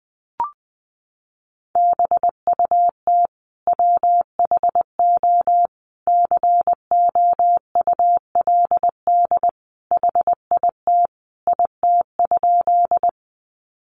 Morse code, heard as a keyed tone, about 20 wpm, 700 Hz. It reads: BUT WHO COULD HIT IT?
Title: Sentences from Top 1200 Words 20wpm